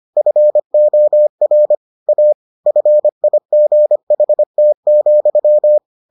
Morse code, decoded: FOR A FIGHT,

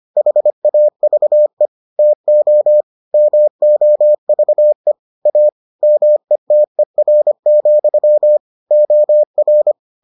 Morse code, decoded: HAVE TO MOVE A METER, OR

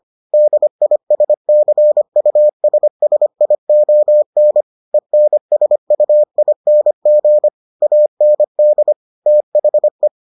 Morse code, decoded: DISCUSSION ENSUING AND THE

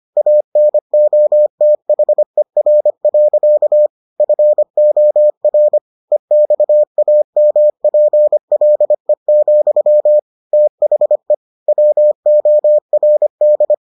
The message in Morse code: ANOTHER. FOR EXAMPLE, THE WORD

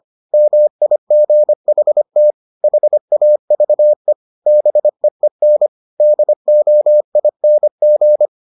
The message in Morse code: MIGHT HAVE BEEN DOING

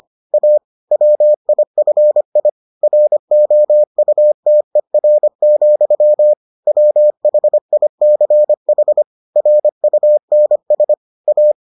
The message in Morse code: A WIFI ROUTER, WHICH RUNS A

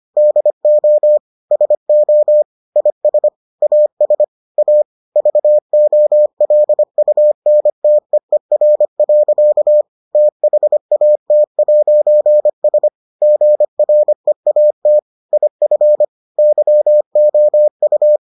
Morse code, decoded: DO SO IS AS A VOLUNTEER. THAT'S GREAT IF YOU

